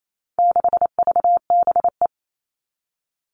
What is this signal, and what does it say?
Morse code, heard as a keyed tone, about 28 wpm, 700 Hz. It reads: 6VBE
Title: All Cut Numbers - 4-Digit 28wpm